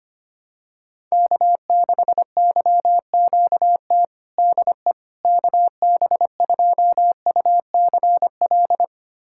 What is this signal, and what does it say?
Morse code, heard as a keyed tone, about 25 wpm, 700 Hz. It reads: K6YQT DE KB2UCL
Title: USA Callsign DE Callsign 25wpm